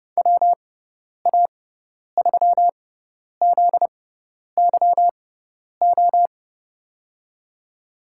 Morse code, heard as a keyed tone, about 30 wpm, spaced as for 10 wpm, 700 Hz. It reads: WA3ZYO
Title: USA Call Signs - 6x Character Spacing - 30wpm